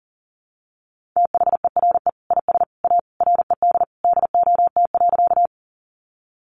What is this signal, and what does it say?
Morse code, heard as a keyed tone, about 37 wpm, 700 Hz. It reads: THERE IS A RED DOT.
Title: Sentences from Top 1500 Words 40wpm